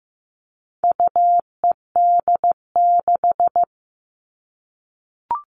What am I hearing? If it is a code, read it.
Morse code, heard as a keyed tone, about 15 wpm, 700 Hz. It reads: UED6
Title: All Cut Numbers - 4-Digit 15wpm